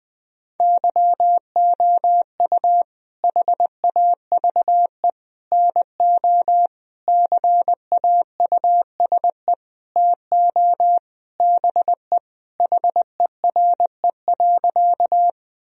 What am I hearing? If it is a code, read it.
Morse code, heard as a keyed tone, about 20 wpm, 700 Hz. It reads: YOU HAVE NO CAUSE TO BE HERE.
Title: Sentences from Top 1000 Words - Review Part 1 - 20wpm